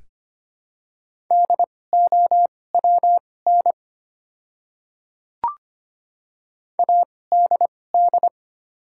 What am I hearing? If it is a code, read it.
Morse code, heard as a keyed tone, about 25 wpm, spaced as for 18 wpm, 700 Hz. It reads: DOWN ADD
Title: 2x Character Spacing - Top 500 Words - 25wpm